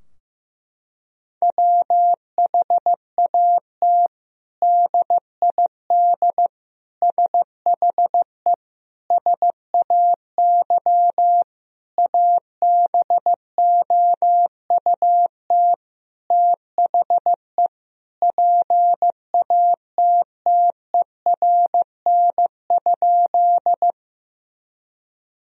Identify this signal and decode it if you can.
Morse code, heard as a keyed tone, about 15 wpm, 700 Hz. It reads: WHAT DID SHE SAY ABOUT THE PATTERN?
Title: Sentences from Top 1000 Words - Review Part 2 - 15wpm